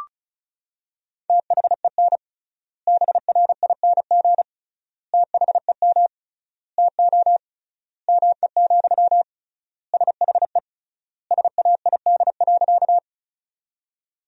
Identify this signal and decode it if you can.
Morse code, heard as a keyed tone, about 36 wpm, 700 Hz. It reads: THEN BRING THEM TO ME, SHE SAID.
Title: 3x Word-Spacing - Sentences from Top 500 Words - 35wpm